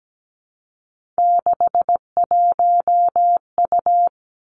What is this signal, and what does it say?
Morse code, heard as a keyed tone, about 17 wpm, 700 Hz. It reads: 61U
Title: Three Challenging Letter-Number Combinations 17wpm